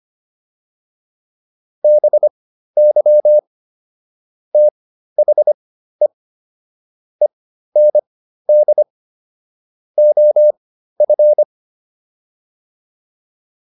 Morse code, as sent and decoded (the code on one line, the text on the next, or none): -... -.-- / - .... / . -. -.. / --- ..-.
BY THE END OF